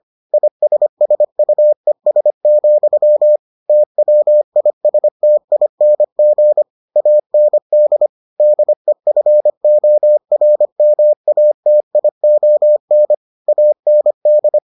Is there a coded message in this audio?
Yes: Morse code from 0.3 to 14.6 s, heard as a keyed tone, about 25 wpm, 600 Hz. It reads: ISSUES, TWISTING AND DEFORMATION AND